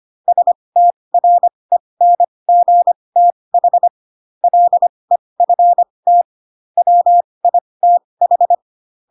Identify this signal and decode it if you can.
Morse code, heard as a keyed tone, about 25 wpm, spaced as for 20 wpm, 700 Hz. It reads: STRENGTH LEFT WITH
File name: Pinocchio / Pinocchio ch6 15 25 ik1hge